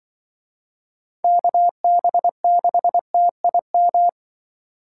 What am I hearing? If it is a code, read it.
Morse code, heard as a keyed tone, about 24 wpm, 700 Hz. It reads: KB6TIM
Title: Speed Racing - USA Call Signs 20wpm